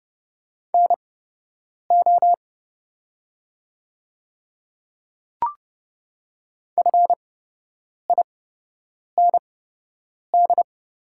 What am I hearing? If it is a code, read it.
Morse code, heard as a keyed tone, about 30 wpm, spaced as for 8 wpm, 700 Hz. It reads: NO FIND
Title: 8x Character Spacing - Top 100 Words - 30wpm